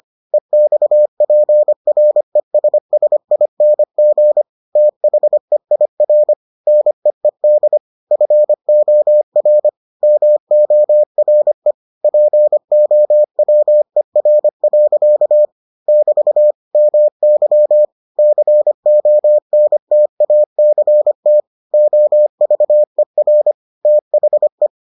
Morse code, heard as a keyed tone, about 25 wpm, 600 Hz. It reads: EXPRESSING THEIR NEED FOR MORE POWER. <BT> MY CONTACT OVER THE